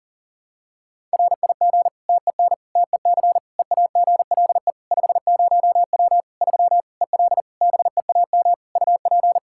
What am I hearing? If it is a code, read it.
Morse code, heard as a keyed tone, about 40 wpm, 700 Hz. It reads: RIG TEN TEC EAGLE 50W 3 EL BEAM UP
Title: Practice QSO - Exchange 2 - Info - 40wpm